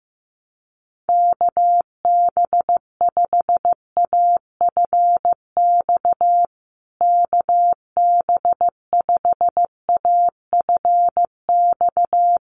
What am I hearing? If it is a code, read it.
Morse code, heard as a keyed tone, about 15 wpm, 700 Hz. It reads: KB5AFX KB5AFX
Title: USA Call Signs Repeated 3x 15wpm